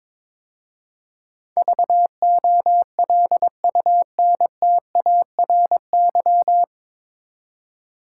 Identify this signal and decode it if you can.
Morse code, heard as a keyed tone, about 22 wpm, 700 Hz. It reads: VOLUNTARY
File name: 9 Letter Words - First 3 Letters Given - 22wpm